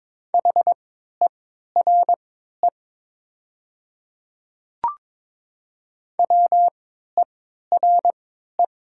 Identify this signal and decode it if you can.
Morse code, heard as a keyed tone, about 22 wpm, spaced as for 12 wpm, 700 Hz. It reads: HERE WERE
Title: 3x Character Spacing - Top 300 Words - 22wpm